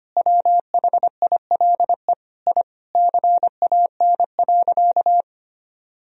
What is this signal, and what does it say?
Morse code, heard as a keyed tone, about 25 wpm, 700 Hz. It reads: WHILE I CAN.
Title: Sentences from Top 1000 Words - Review Part 1 - 25wpm